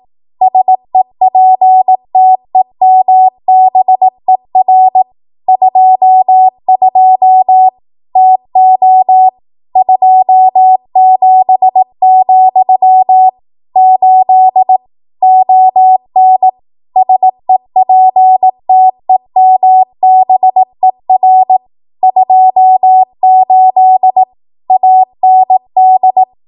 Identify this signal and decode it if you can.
Morse code, heard as a keyed tone, about 18 wpm, 750 Hz. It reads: SEPTEMBER 22 TO 27, 8 ON SEPTEMBER 28 AND
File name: Archive / 18 WPM / 18WPM